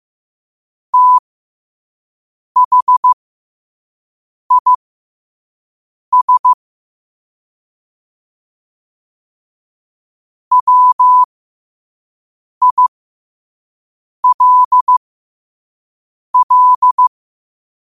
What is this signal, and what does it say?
Morse code, heard as a keyed tone, about 15 wpm, spaced as for 5 wpm, 1000 Hz. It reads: THIS WILL
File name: qotdfiles / 5wpm